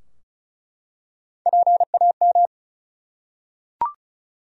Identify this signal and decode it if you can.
Morse code, heard as a keyed tone, about 36 wpm, 700 Hz. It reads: PAM
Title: USA Common Names - Max 4 Letters - 35wpm